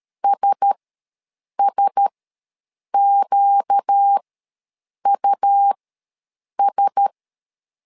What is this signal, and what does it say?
Morse code, heard as a keyed tone, about 13 wpm, spaced as for 7 wpm, 800 Hz. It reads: SSQUS
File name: LEZIONI / 4 Q1S ws 12wpm 1827